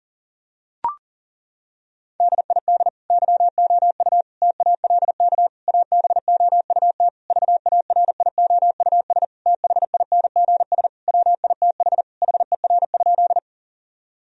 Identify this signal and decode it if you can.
Morse code, heard as a keyed tone, about 40 wpm, 690 Hz. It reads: DID YOU TALK ABOUT VARIOUS THINGS WITH HER?
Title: Sentences from Top 1000 Words - Review Part 4 - 40wpm